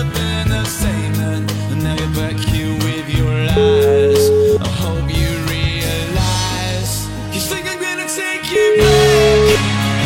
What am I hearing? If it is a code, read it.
Ringback tone (Europe, 425 Hz), with noise.